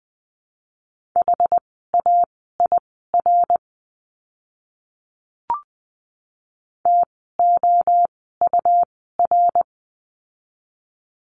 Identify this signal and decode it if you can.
Morse code, heard as a keyed tone, about 20 wpm, spaced as for 15 wpm, 700 Hz. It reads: HAIR TOUR
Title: 101 Lesson 17f - Letter u - Words 20wpm